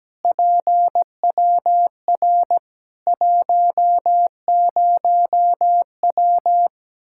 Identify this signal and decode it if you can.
Morse code, heard as a keyed tone, about 17 wpm, 700 Hz. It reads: PWR 10W
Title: K1USN SST QSO Practice 17wpm